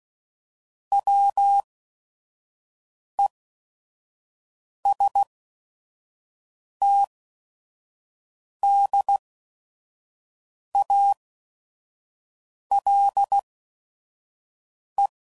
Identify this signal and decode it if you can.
Morse code, heard as a keyed tone, about 16 wpm, spaced as for 5 wpm, 800 Hz. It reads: WESTDALE
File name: CW Code Groups @5WPM / fromSM's / Cities